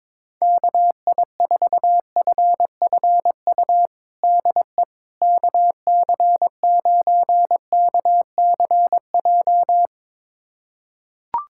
Morse code, heard as a keyed tone, about 21 wpm, 700 Hz. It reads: KI4FFU DE KC9KCJ
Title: USA Callsign DE Callsign 22wpm